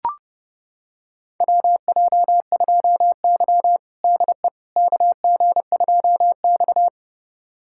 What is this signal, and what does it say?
Morse code, heard as a keyed tone, about 32 wpm, 700 Hz. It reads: WJ2Y DE KG2X
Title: USA Callsign DE Callsign - 4 Characters - 30wpm